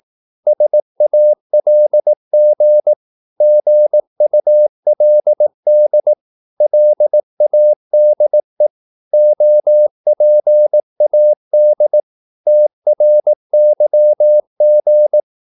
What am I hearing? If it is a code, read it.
Morse code, heard as a keyed tone, about 18 wpm, 600 Hz. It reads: SALG GULD LADE OPAD TRYG
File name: Korte ord DK 18wpm